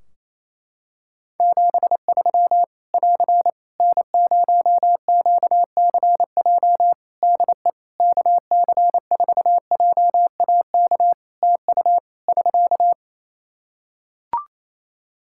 Morse code, heard as a keyed tone, about 27 wpm, 700 Hz. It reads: 73 <AR> N0QCJ DE KC4JAK TU <SK>
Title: K1USN SST QSO Practice 28wpm